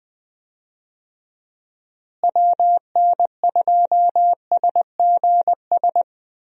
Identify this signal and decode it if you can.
Morse code, heard as a keyed tone, about 20 wpm, 700 Hz. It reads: WN2SGS
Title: USA Call Signs - 6 Characters - Repeated 3x - 20wpm